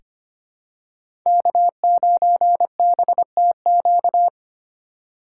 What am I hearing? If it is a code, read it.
Morse code, heard as a keyed tone, about 25 wpm, 700 Hz. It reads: K9BTQ
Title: K1USN SST QSO Practice - 25wpm